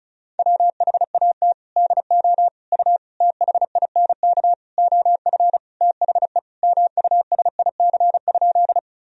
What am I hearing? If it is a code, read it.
Morse code, heard as a keyed tone, about 36 wpm, 700 Hz. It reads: WHAT DO U THINK OF THE MUSIC?
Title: Sentences from Top 300 Words and Abbreviations 35wpm